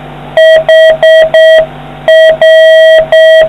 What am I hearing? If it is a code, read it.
Morse code, heard as a keyed tone, about 6 wpm, 1950 Hz, with noise. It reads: HR